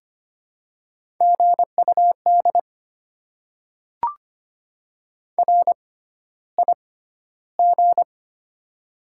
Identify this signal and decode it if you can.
Morse code, heard as a keyed tone, about 25 wpm, spaced as for 13 wpm, 700 Hz. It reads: GUD RIG
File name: Top 20 QSO Elements - 6x Character Spacing - 25wpm